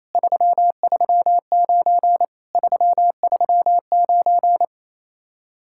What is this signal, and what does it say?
Morse code, heard as a keyed tone, about 27 wpm, 700 Hz. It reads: 339 339
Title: Speed-Racing Common QSO Phrases - 28wpm